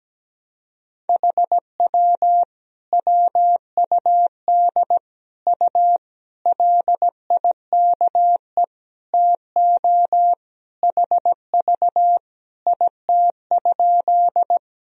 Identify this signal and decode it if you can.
Morse code, heard as a keyed tone, about 17 wpm, 700 Hz. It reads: HW WUD U LIKE TO HV IT?
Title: Sentences from Top 100 Words and Abbreviations 17wpm